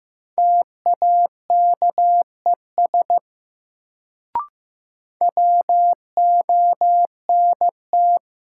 Morse code, heard as a keyed tone, about 15 wpm, 700 Hz. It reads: TAKES WONT
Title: Sentences from Top 1000 Words - Review Part 6 - New Word Drill - 15wpm